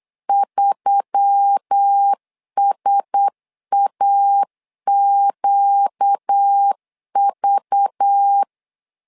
Morse code, heard as a keyed tone, about 8 wpm, 800 Hz. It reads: 3SAQV
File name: LEZIONI / 5 LBVC / 5 LBV 8wpm 2837